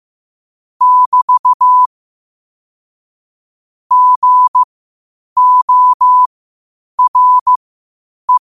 Morse code, heard as a keyed tone, about 15 wpm, spaced as for 8 wpm, 1000 Hz. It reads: <BT> GORE
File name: qotdfiles / 8wpm